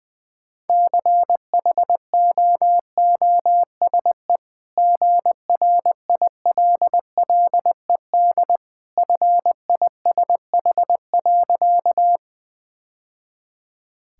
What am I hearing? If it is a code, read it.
Morse code, heard as a keyed tone, about 21 wpm, 700 Hz. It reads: CHOOSE GRILLED FISH.
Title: Easy Sentences with Hints - 20wpm